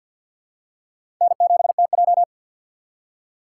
Morse code, heard as a keyed tone, about 48 wpm, 700 Hz. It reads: N7TJ
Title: USA Call Signs - 4 Characters - 6x Character Spacing - 50wpm